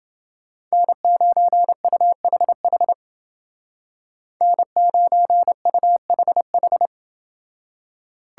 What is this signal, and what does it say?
Morse code, heard as a keyed tone, about 28 wpm, 700 Hz. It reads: N9UHH N9UHH